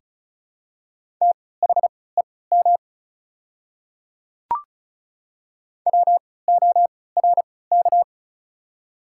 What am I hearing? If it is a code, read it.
Morse code, heard as a keyed tone, about 34 wpm, spaced as for 20 wpm, 700 Hz. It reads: THEM WORK
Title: 3x Character Spacing - Top 200 Words - 35wpm